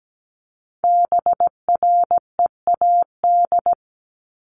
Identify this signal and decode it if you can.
Morse code, heard as a keyed tone, about 17 wpm, 700 Hz. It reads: BREAD
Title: Top 1501-1750 Words 17wpm